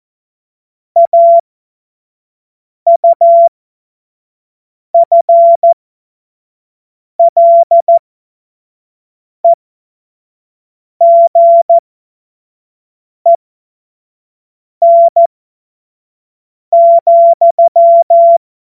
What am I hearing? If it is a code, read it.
Morse code, heard as a keyed tone, about 14 wpm, spaced as for 5 wpm, 680 Hz. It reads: AUFLEGEN,
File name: Morsekurs / Witze / Witz 03. Januar 2023 25 F70